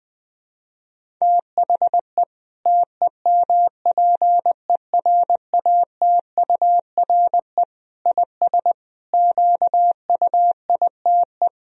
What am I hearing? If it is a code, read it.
Morse code, heard as a keyed tone, about 20 wpm, 700 Hz. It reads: THE TEMPERATURE IS QUITE
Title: Sentences from Top 1000 Words - Review Part 4 - 20wpm